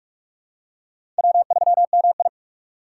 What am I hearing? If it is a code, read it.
Morse code, heard as a keyed tone, about 46 wpm, 700 Hz. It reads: W3MI
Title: USA Call Signs - 4 Characters - 4x Character Spacing - 45wpm